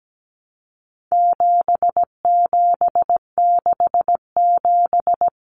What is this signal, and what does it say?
Morse code, heard as a keyed tone, about 17 wpm, 700 Hz. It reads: 7767
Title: Numbers - 4-Digit 17wpm